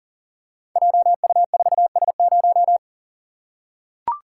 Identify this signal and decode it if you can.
Morse code, heard as a keyed tone, about 37 wpm, 700 Hz. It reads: JU4S0
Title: Code Groups - 5 Letter:Numbers - 40wpm